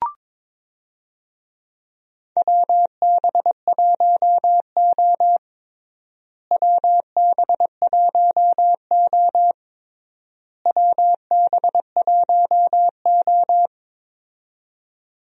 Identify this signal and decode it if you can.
Morse code, heard as a keyed tone, about 22 wpm, 700 Hz. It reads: WB1O WB1O WB1O
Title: USA Call Signs - 4 Characters - Repeated 3x - 22wpm